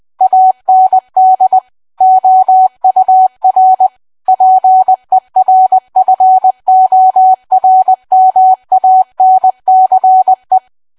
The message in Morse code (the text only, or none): AND OUR PERFORMANCE